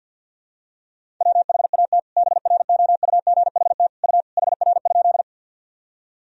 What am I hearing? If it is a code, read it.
Morse code, heard as a keyed tone, about 52 wpm, 700 Hz. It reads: WHAT BROUGHT U HR?
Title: Sentences from Top 500 Words and Abbreviations 50wpm